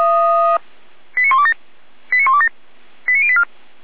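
Five-tone selective call (ZVEI), heard as quick runs of five tones, tones of about 70 ms. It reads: 89127 89127 89084
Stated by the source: ZVEI five-tone call received by SDR.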